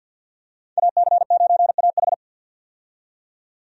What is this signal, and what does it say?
Morse code, heard as a keyed tone, about 48 wpm, 700 Hz. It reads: AC9AH
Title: USA Call Signs - 5 Characters - 50wpm